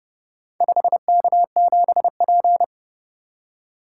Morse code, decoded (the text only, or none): 5K7P